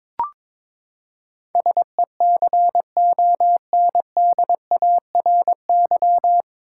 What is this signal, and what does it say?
Morse code, heard as a keyed tone, about 22 wpm, 700 Hz. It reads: SECONDARY
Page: Morse Code Ninja practice 22wpm